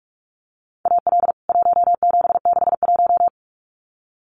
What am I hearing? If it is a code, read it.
Morse code, heard as a keyed tone, about 44 wpm, 700 Hz. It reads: AL 1761
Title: CWT Exchange 45wpm